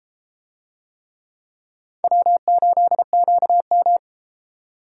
Morse code, heard as a keyed tone, about 31 wpm, 700 Hz. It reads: W8QM